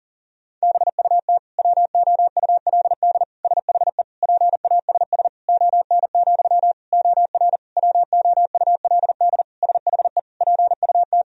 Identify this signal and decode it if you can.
Morse code, heard as a keyed tone, about 40 wpm, 700 Hz. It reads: BUT WOULD SHE PASS ON, OR WOULD SHE PUT
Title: Sentences from Top 1000 Words - Review Part 1 - 40wpm